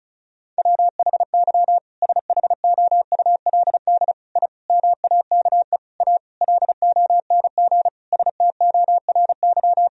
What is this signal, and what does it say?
Morse code, heard as a keyed tone, about 36 wpm, 700 Hz. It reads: WHY SHOULD I MAKE A LONG STORY
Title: Sentences from Top 1000 Words - Review Part 1 - 35wpm